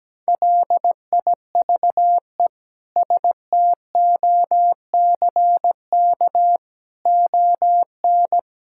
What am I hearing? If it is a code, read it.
Morse code, heard as a keyed tone, about 17 wpm, 700 Hz. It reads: LIVE STOCK ON
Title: Sentences from Top 1000 Words - Review Part 5 - 17wpm